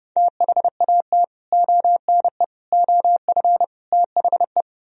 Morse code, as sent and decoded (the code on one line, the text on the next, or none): - .... .- - / --- -. / --- ..-. / - .... .
THAT ONE OF THE